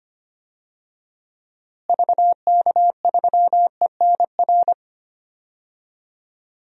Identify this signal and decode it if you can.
Morse code, heard as a keyed tone, about 25 wpm, 700 Hz. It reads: VK3ENR
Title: DXCC-100 Call Signs - 6 Characters - Repeated 3x - 25wpm